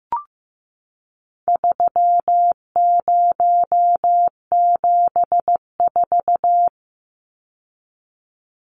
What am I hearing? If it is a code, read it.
Morse code, heard as a keyed tone, about 15 wpm, 700 Hz. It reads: 3074
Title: Numbers - 4-Digit 15wpm